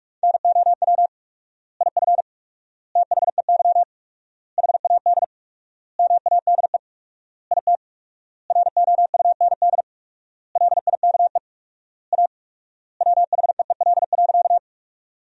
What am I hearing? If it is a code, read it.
Morse code, heard as a keyed tone, about 46 wpm, 700 Hz. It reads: NOW IF THEY HAD MADE IT ROUND LIKE A WHEEL.
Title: 4x Word-Spacing - Sentences from Top 500 Words - 45wpm